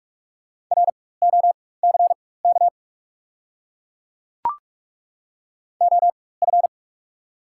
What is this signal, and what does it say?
Morse code, heard as a keyed tone, about 43 wpm, spaced as for 21 wpm, 690 Hz. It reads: ROCK OF
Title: 4x Character Spacing - Top 500 Words - 45wpm